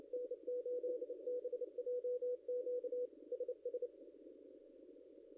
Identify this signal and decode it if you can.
Morse code, heard as a keyed tone, about 31 wpm, 490 Hz. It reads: N8BJQ SS
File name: contests / ss-cw-k5tr-op-n5rz -0700